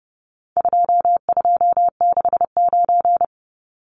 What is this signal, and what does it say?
Morse code, heard as a keyed tone, about 30 wpm, 700 Hz. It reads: 2269
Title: Numbers - 4-Digit 30wpm